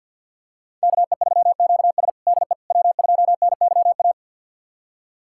Morse code, heard as a keyed tone, about 49 wpm, 700 Hz. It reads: KE3QS DE W2NYA
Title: USA Callsign DE Callsign - 5 Characters - 50wpm